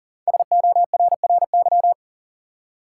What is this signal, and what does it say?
Morse code, heard as a keyed tone, about 40 wpm, 700 Hz. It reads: SORRY
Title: Five Letter Words - 40wpm